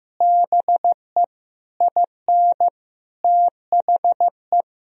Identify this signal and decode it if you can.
Morse code, heard as a keyed tone, about 15 wpm, 700 Hz. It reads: BE IN THE
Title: Sentences from Top 1000 Words - Review Part 2 - 15wpm